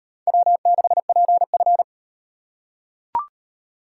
Morse code, heard as a keyed tone, about 38 wpm, 700 Hz. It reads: W6PF